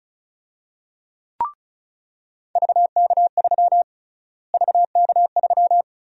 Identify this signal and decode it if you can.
Morse code, heard as a keyed tone, about 36 wpm, 700 Hz. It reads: VK3 VK3
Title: USA Call Sign Prefixes - Repeated 2x - 35wpm